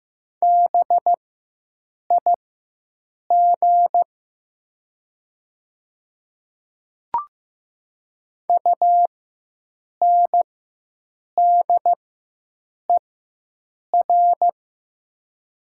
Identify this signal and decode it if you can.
Morse code, heard as a keyed tone, about 15 wpm, spaced as for 7 wpm, 700 Hz. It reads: BIG UNDER
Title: 4x Character Spacing - Top 500 Words - 15wpm